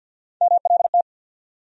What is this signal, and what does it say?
Morse code, heard as a keyed tone, about 51 wpm, 700 Hz. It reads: MLT
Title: USA Call Sign Postfixes - 3 Letters 50wpm